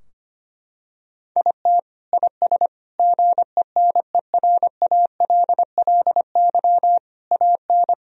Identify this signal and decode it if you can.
Morse code, heard as a keyed tone, about 25 wpm, 700 Hz. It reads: IT IS GENERALLY AN
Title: Sentences from Top 1000 Words - Review Part 4 - 25wpm